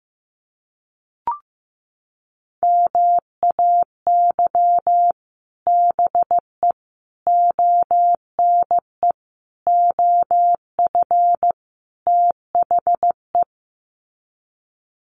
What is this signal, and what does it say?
Morse code, heard as a keyed tone, about 15 wpm, 700 Hz. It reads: MAY BE ONE OF THE
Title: Sets of 5 Words Encore2 15wpm